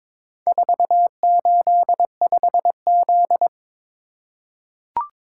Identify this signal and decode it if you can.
Morse code, heard as a keyed tone, about 22 wpm, 700 Hz. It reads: 485Z
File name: practice / 22wpm / Four Letter-Number Combinations - 22wpm